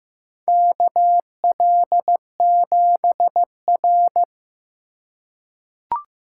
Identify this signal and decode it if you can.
Morse code, heard as a keyed tone, about 15 wpm, 700 Hz. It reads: KL7R
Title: USA Call Signs - 4 Characters - 3x Character Spacing - 15wpm